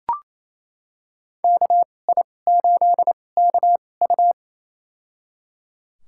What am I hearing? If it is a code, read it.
Morse code, heard as a keyed tone, about 28 wpm, spaced as for 20 wpm, 700 Hz. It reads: KI8KU
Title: USA Call Signs - 2x Character Spacing - 28wpm